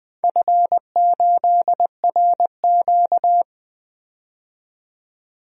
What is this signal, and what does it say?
Morse code, heard as a keyed tone, about 20 wpm, 700 Hz. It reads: F8RQ
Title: DXCC-100 Call Signs - 4 Characters - 20wpm